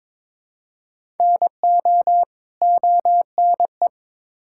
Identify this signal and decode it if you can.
Morse code, heard as a keyed tone, about 22 wpm, 700 Hz. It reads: NO ONE
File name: Build A Phrase - 2-Word Phrases - 22wpm